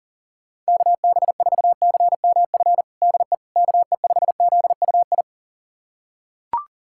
Morse code, heard as a keyed tone, about 40 wpm, 700 Hz. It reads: KB4CMF DE KE5ZUI
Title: USA Callsign DE Callsign 40wpm